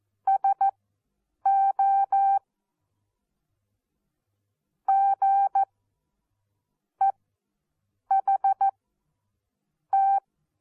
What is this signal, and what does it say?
Morse code, heard as a keyed tone, about 14 wpm, spaced as for 6 wpm, 780 Hz. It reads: SO GEHT
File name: Onlinelehrgang / morsen / DJ4UF Morsen14